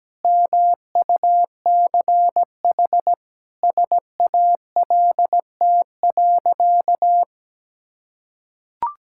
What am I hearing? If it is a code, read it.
Morse code, heard as a keyed tone, about 17 wpm, 700 Hz. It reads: MUCH SALT.
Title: Sentences from Top 1000 Words - Review Part 5 - 17wpm